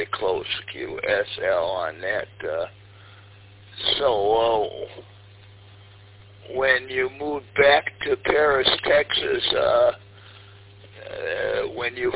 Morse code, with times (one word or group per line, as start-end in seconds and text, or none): none